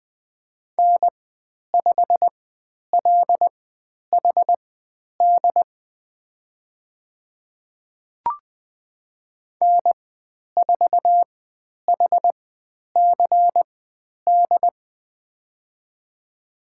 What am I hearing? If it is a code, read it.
Morse code, heard as a keyed tone, about 20 wpm, spaced as for 10 wpm, 700 Hz. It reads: N5LHD N4HCD
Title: 109 Lesson 18g - Letter m - Call Signs 20wpm